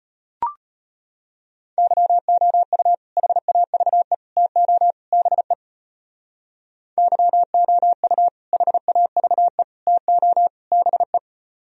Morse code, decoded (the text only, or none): YOU HAVE TO BE YOU HAVE TO BE